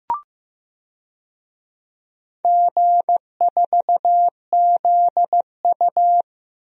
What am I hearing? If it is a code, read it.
Morse code, heard as a keyed tone, about 15 wpm, 700 Hz. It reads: G4ZU
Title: DXCC-100 Call Signs - 4 Characters - 15wpm